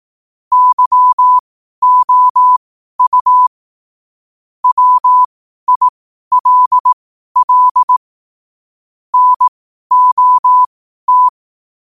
Morse code, heard as a keyed tone, about 18 wpm, spaced as for 12 wpm, 1000 Hz. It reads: YOU WILL NOT
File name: qotdfiles / 12wpm